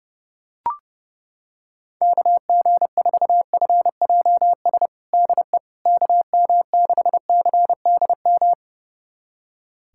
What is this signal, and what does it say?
Morse code, heard as a keyed tone, about 32 wpm, 700 Hz. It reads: KG4FJS DE KM6CDM
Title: USA Callsign DE Callsign 30wpm